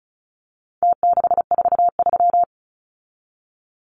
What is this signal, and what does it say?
Morse code, heard as a keyed tone, about 34 wpm, 700 Hz. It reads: T643
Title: Numbers - Contest Style - 4-Digit 35wpm